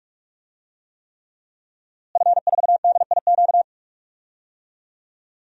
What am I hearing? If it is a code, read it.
Morse code, heard as a keyed tone, about 44 wpm, 700 Hz. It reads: F4DIQ
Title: DXCC-100 Call Signs - 5 Characters - Repeated 3x - 45wpm